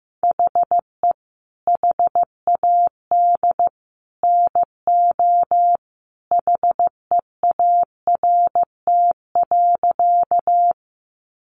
Morse code, heard as a keyed tone, about 15 wpm, 700 Hz. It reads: HE HAD NO HEART.
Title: Sentences from Top 1200 Words 15wpm